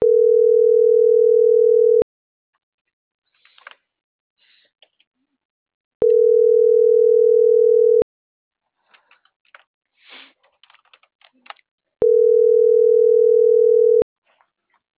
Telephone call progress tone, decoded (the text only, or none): ringback tone